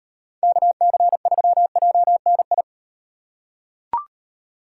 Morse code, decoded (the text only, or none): KC3JNI